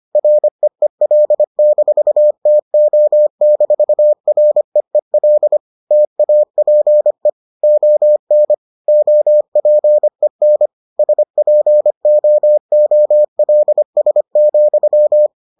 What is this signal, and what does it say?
Morse code, heard as a keyed tone, about 25 wpm, 600 Hz. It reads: REEL-TO-REEL TAPE ON OPEN SPOOLS,